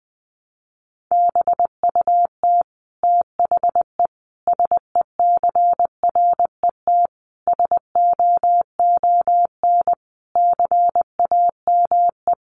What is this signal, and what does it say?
Morse code, heard as a keyed tone, about 20 wpm, 700 Hz. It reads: BUT THE SECRET SOON CAME
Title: Sentences from Top 1200 Words 20wpm